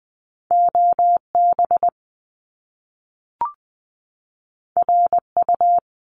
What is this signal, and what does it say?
Morse code, heard as a keyed tone, about 20 wpm, 700 Hz. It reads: OB RU
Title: Top Two Letter Combinations in Words 20wpm